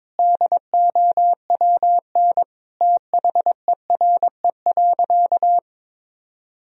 Morse code, heard as a keyed tone, about 21 wpm, 700 Hz. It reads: DOWN THERE.
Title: Sentences from Top 1000 Words - Review Part 4 - 22wpm